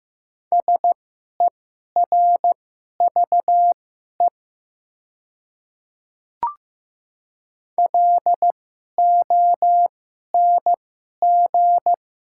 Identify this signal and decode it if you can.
Morse code, heard as a keyed tone, about 15 wpm, spaced as for 11 wpm, 700 Hz. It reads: SERVE LONG
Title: 2x Character Spacing - Top 500 Words - 15wpm